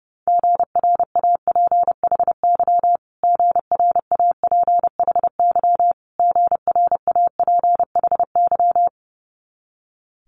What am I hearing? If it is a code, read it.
Morse code, heard as a keyed tone, about 30 wpm, 700 Hz. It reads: GRAPHY GRAPHY GRAPHY
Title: Multi-letter Phonemes Repeated 3x 30wpm